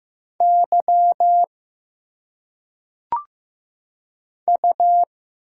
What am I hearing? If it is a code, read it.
Morse code, heard as a keyed tone, about 15 wpm, 700 Hz. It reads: Y U
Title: Single Letters - Phonetic Pronunciation 15wpm